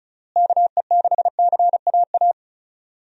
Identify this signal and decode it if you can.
Morse code, heard as a keyed tone, about 36 wpm, 700 Hz. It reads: KE6CAA
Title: USA Call Signs - 6 Characters - Repeated 3x - 35wpm